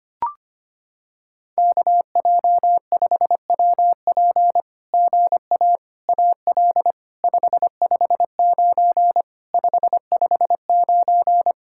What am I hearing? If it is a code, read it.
Morse code, heard as a keyed tone, about 25 wpm, 700 Hz. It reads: KJ5WP GA AL 559 559
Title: SOTA QSO Practice 25wpm